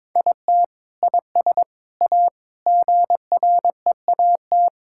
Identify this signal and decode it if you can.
Morse code, heard as a keyed tone, about 22 wpm, 700 Hz. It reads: IT IS A GREAT
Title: Sentences from Top 1000 Words - Review Part 5 - 22wpm